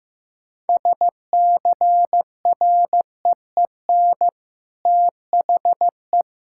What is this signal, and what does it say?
Morse code, heard as a keyed tone, about 15 wpm, 700 Hz. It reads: SCREEN THE
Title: Sentences from Top 1000 Words - Review Part 6 - 15wpm